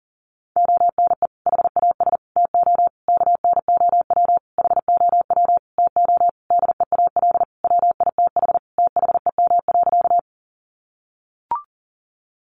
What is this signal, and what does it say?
Morse code, heard as a keyed tone, about 40 wpm, 700 Hz. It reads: ONE HAS TO KNOW HOW TO DEAL WITH THEM.
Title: Sentences from Top 800 Words 40wpm